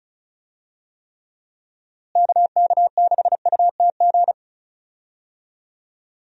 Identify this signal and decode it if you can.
Morse code, heard as a keyed tone, about 34 wpm, 700 Hz. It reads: KK6UTG
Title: USA Call Signs - 6 Characters - Repeated 3x - 35wpm